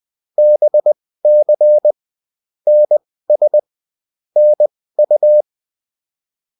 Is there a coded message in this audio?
Yes: Morse code from 0.4 to 5.4 s, heard as a keyed tone, about 20 wpm, spaced as for 15 wpm, 600 Hz. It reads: BC NS NU